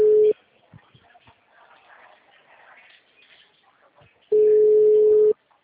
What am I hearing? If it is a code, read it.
Ringback tone (Europe, 425 Hz).